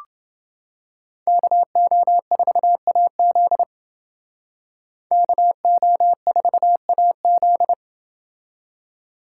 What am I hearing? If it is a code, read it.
Morse code, heard as a keyed tone, about 29 wpm, 700 Hz. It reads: KO4AZ KO4AZ